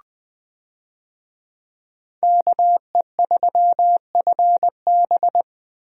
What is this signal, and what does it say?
Morse code, heard as a keyed tone, about 20 wpm, 700 Hz. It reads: KE3FB
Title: USA Call Signs - 5 Characters - Repeated 2x - 20wpm